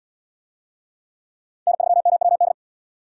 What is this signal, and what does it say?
Morse code, heard as a keyed tone, about 72 wpm, 700 Hz. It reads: N3MWD